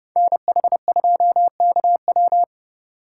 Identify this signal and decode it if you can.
Morse code, heard as a keyed tone, about 31 wpm, 700 Hz. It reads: NH2KW
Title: USA Call Signs - 5 Characters - 2x Character Spacing - 30wpm